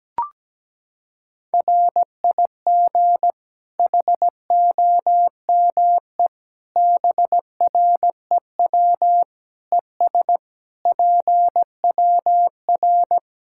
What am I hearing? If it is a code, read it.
Morse code, heard as a keyed tone, about 17 wpm, 700 Hz. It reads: RIG HOME BREW ES PWR
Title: Practice QSO - Exchange 2 - Info - 17wpm